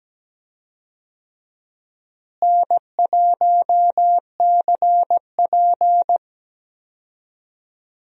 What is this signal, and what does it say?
Morse code, heard as a keyed tone, about 17 wpm, 700 Hz. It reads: N1CP
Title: USA Call Signs - 4 Characters - Repeated 2x - 17wpm